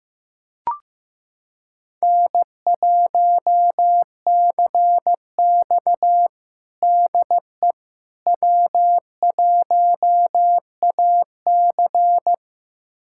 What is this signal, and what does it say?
Morse code, heard as a keyed tone, about 15 wpm, 700 Hz. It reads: N1CX DE W1AC
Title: USA Callsign DE Callsign - 4 Characters - 15wpm